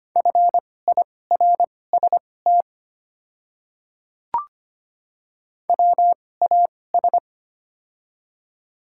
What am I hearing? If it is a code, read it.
Morse code, heard as a keyed tone, about 25 wpm, spaced as for 18 wpm, 700 Hz. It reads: FIRST WAS
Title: 2x Character Spacing - Top 100 Words - 25wpm